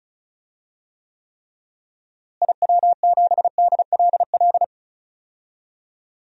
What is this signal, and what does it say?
Morse code, heard as a keyed tone, about 36 wpm, 700 Hz. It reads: IW7DLL
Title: DXCC-100 Call Signs - 6 Characters - Repeated 3x - 35wpm